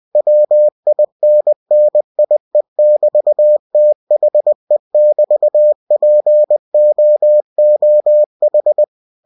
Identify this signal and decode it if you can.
Morse code, heard as a keyed tone, about 20 wpm, 600 Hz. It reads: WINNIE<BT>THE<BT>POOH
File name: cw-books / winnie-the-pooh / winnie-20wpm-page-0029